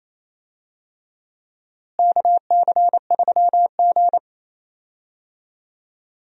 Morse code, heard as a keyed tone, about 28 wpm, 700 Hz. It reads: KC3G
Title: USA Call Signs - 4 Characters - Repeated 3x - 28wpm